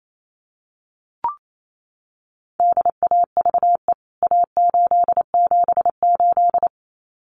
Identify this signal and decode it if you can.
Morse code, heard as a keyed tone, about 28 wpm, 700 Hz. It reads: DAVE A878
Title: CWT Exchange 28wpm